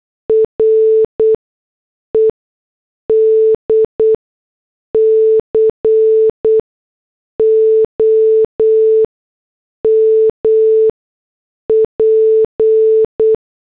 Morse code, heard as a keyed tone, about 8 wpm, spaced as for 6 wpm, 440 Hz. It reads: RED COMP
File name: morse / headlines